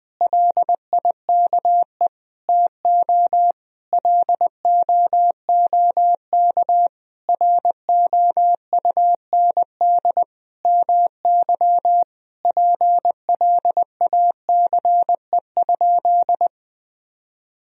Morse code, heard as a keyed tone, about 20 wpm, 700 Hz. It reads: LIKE TO LOOK ROUND MY PLACE?
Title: Sentences from Top 200 Words and Abbreviations 20wpm